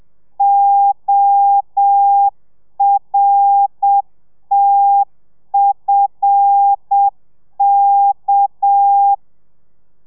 Morse code, heard as a keyed tone, about 7 wpm, 800 Hz. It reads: ORTFK